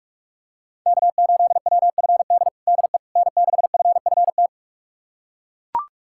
Morse code, heard as a keyed tone, about 42 wpm, 700 Hz. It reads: K8WFD DE N6FFT
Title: USA Callsign DE Callsign - 5 Characters - 45wpm